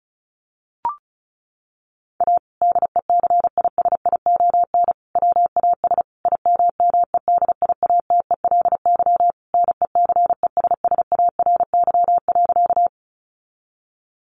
Morse code, heard as a keyed tone, about 34 wpm, 700 Hz. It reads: A DECISION WAS IMMEDIATELY NECESSARY.
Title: Sentences from Top 1100 Words 35wpm